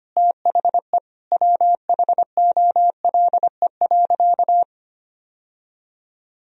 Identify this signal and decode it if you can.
Morse code, heard as a keyed tone, about 25 wpm, 700 Hz. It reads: THE WHOLE.
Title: Sentences from Top 1000 Words - Review Part 1 - 25wpm